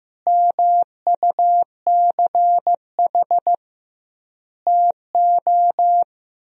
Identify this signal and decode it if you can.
Morse code, heard as a keyed tone, about 16 wpm, 700 Hz. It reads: MUCH TO
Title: 2x Word-Spacing - Sentences from Top 400 Words - 15wpm